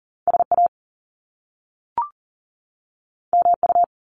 Morse code, heard as a keyed tone, about 38 wpm, 700 Hz. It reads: SA MU
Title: Top Two Letter Combinations in Words 40wpm